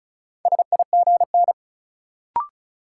Morse code, heard as a keyed tone, about 36 wpm, 700 Hz. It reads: SIGN
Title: Sentences from Top 1000 Words - New Word Drill - 35wpm